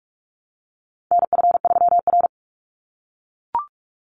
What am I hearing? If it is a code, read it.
Morse code, heard as a keyed tone, about 44 wpm, 700 Hz. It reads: NF3R
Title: Call Signs - 45wpm